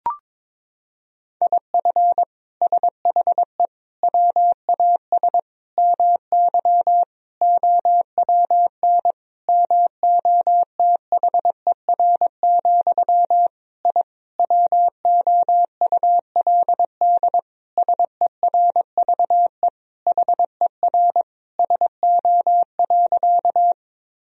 Morse code, heard as a keyed tone, about 21 wpm, 700 Hz. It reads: IF SHE WAS MY OWN MOTHER, I WOULD SERVE HER SO.
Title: Sentences from Top 1000 Words - Review Part 2 - 22wpm